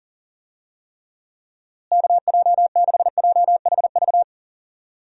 Morse code, heard as a keyed tone, about 40 wpm, 700 Hz. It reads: KJ6JHV
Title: USA Call Signs - 6 Characters - Repeated 2x - 40wpm